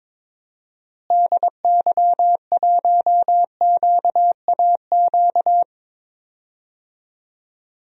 Morse code, heard as a keyed tone, about 22 wpm, 700 Hz. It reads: DY1QAQ
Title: DXCC-100 Call Signs - 6 Characters - Repeated 2x - 22wpm